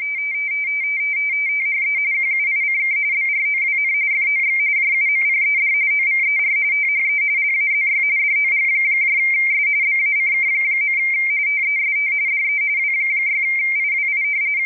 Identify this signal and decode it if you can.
RTTY, about 45.45 baud. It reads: THE QUICK BROWN FOX JUMPED OVER THE LAZY DOGS BACK. TEST DE W6IWI
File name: rtty / websdr recording 14086.0kHz